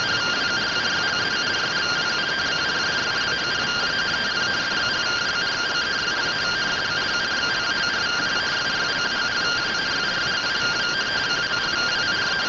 Two-tone teleprinter signal, about 170 Hz shift.